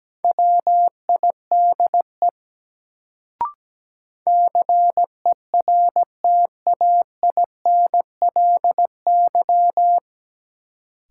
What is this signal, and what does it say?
Morse code, heard as a keyed tone, about 17 wpm, 700 Hz. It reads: WIDE CERTAINLY
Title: Sentences from Top 900 Words - New Word Drill - 17wpm